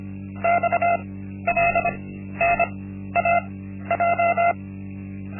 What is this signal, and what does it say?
Morse code, heard as a keyed tone, about 25 wpm, spaced as for 13 wpm, 690 Hz. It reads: XLNAJ
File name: Linksmakalnis-1